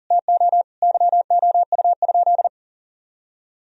Morse code, heard as a keyed tone, about 40 wpm, 700 Hz. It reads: TO YOU?
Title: Sentences from Top 1000 Words - Review Part 3 - 40wpm